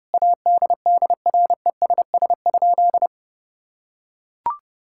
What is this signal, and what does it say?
Morse code, heard as a keyed tone, about 30 wpm, 700 Hz. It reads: ADDRESS?
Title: Sentences from Top 1000 Words - Review Part 6 - 30wpm